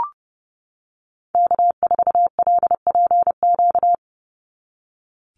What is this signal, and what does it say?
Morse code, heard as a keyed tone, about 30 wpm, 700 Hz. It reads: K4LPQ
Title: NA SOTA Activator Callsigns 30wpm